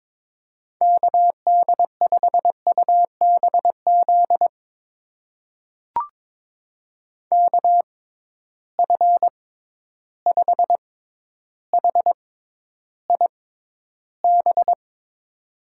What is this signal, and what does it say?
Morse code, heard as a keyed tone, about 22 wpm, spaced as for 11 wpm, 700 Hz. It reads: KD5UBZ KF5HIB
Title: USA Call Signs - 6 Characters - 6x Character Spacing - 22wpm